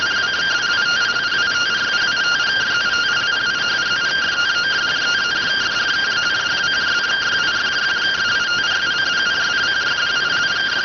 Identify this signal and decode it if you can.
Two-tone teleprinter signal, about 170 Hz shift.